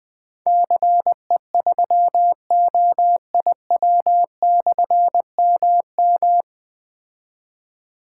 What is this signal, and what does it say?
Morse code, heard as a keyed tone, about 21 wpm, 700 Hz. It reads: CE3OIW/MM
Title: Difficult Call Signs 20wpm